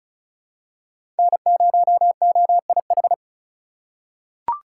Morse code, heard as a keyed tone, about 34 wpm, 700 Hz. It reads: N0OIH
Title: USA Call Signs - 5 Characters - 2x Character Spacing - 35wpm